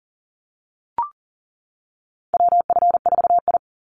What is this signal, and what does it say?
Morse code, heard as a keyed tone, about 39 wpm, 700 Hz. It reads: WF4I
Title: NA SOTA Chaser Callsigns 40wpm